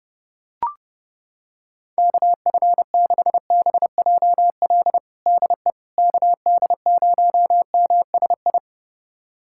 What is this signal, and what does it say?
Morse code, heard as a keyed tone, about 30 wpm, 700 Hz. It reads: KF6BJL DE KD0MSI
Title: USA Callsign DE Callsign - 6 Characters - 30wpm